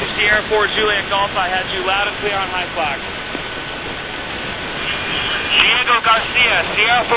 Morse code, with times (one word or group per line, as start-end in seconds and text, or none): none